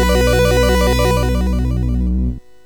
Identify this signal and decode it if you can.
There is no code.